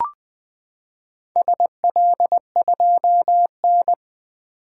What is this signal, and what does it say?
Morse code, heard as a keyed tone, about 20 wpm, 700 Hz. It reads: SL2N
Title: Four Letter-Number Combinations - 20wpm